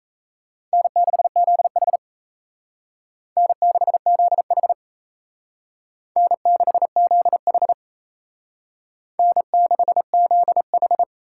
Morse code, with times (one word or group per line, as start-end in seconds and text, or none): 0.7-2.0 N6ZH
3.4-4.7 N6ZH
6.2-7.7 N6ZH
9.2-11.0 N6ZH